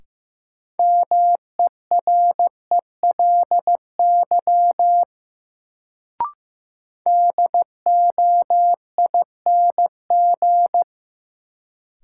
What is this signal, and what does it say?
Morse code, heard as a keyed tone, about 15 wpm, 700 Hz. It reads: MERELY DOING
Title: Sentences from Top 1400 Words - New Word Drill Alt - 15wpm